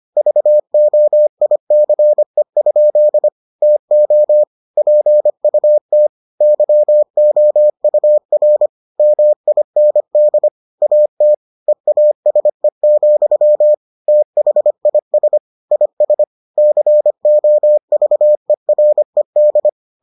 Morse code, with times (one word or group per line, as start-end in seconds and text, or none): 0.2-3.3 VOICE?
3.6-4.4 TO
4.8-6.1 PUT
6.4-8.7 YOUR
9.0-10.5 MIND
10.8-11.3 AT
11.7-13.7 EASE,
14.1-15.4 THIS
15.7-16.2 IS
16.6-19.7 COVERED